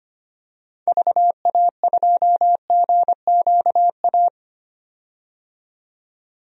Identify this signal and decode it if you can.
Morse code, heard as a keyed tone, about 25 wpm, 700 Hz. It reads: VA2GQA
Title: DXCC-100 Call Signs - 6 Characters - 25wpm